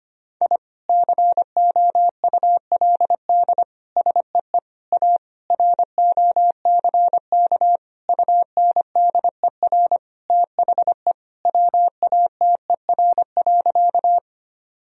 Morse code, heard as a keyed tone, about 25 wpm, 700 Hz. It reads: I COULD SEE A ROCK UNDER THE WATER.
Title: Sentences from Top 1000 Words - Review Part 2 - 25wpm